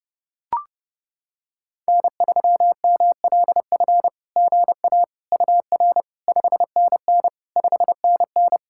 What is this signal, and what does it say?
Morse code, heard as a keyed tone, about 30 wpm, 700 Hz. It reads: N3MLF GA UR 5NN 5NN
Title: SOTA QSO Practice 30wpm